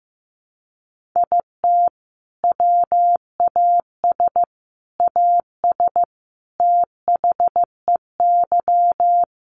Morse code, heard as a keyed tone, about 15 wpm, 700 Hz. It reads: IT WAS AS THEY
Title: Sentences from Top 1500 Words 15wpm